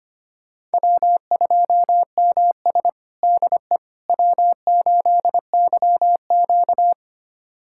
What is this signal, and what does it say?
Morse code, heard as a keyed tone, about 25 wpm, 700 Hz. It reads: W2MS DE W8YQ
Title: USA Callsign DE Callsign - 4 Characters - 25wpm